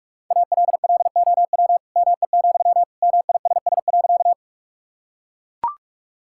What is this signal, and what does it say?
Morse code, heard as a keyed tone, about 41 wpm, 700 Hz. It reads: ALLOW ME, MISS.
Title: Sentences from Top 1000 Words - Review Part 5 - 45wpm